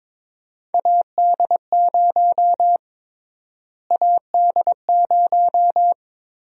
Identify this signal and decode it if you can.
Morse code, heard as a keyed tone, about 22 wpm, 700 Hz. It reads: AD0 AD0
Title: USA Call Sign Prefixes - Repeated 2x - 22wpm